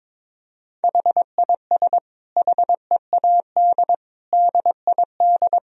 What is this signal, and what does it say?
Morse code, heard as a keyed tone, about 22 wpm, 700 Hz. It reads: HIS HEAD DID
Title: Sentences from Top 1000 Words - Review Part 2 - 22wpm